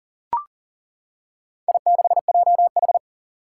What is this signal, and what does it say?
Morse code, heard as a keyed tone, about 40 wpm, 700 Hz. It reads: I6JH
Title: Four Challenging Letter-Number Combinations 40wpm